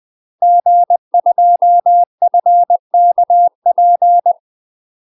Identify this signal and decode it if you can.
Morse code, heard as a keyed tone, about 20 wpm, 700 Hz. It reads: G2FKP
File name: UK MASTER QSOs TRAINING / 20WPM / UK MASTER QSOs 0037